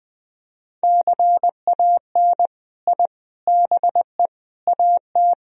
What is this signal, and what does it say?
Morse code, heard as a keyed tone, about 20 wpm, 700 Hz. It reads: CAN I BE AT
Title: Sentences from Top 1000 Words - Review Part 1 - 20wpm